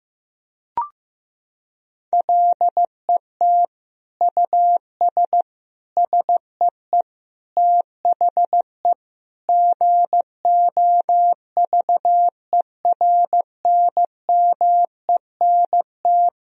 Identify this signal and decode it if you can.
Morse code, heard as a keyed tone, about 15 wpm, 700 Hz. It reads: LET US SEE THE GOVERNMENT
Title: Sentences from Top 1000 Words - Review Part 3 - 15wpm